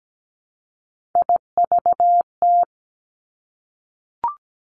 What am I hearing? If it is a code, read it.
Morse code, heard as a keyed tone, about 17 wpm, 700 Hz. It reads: IVT
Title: Three Challenging Letter-Number Combinations 17wpm